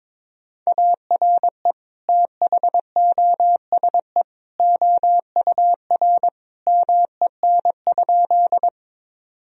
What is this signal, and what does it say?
Morse code, heard as a keyed tone, about 22 wpm, 700 Hz. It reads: ARE THOSE OUR MEN?
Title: Sentences from Top 1000 Words - Review Part 1 - 22wpm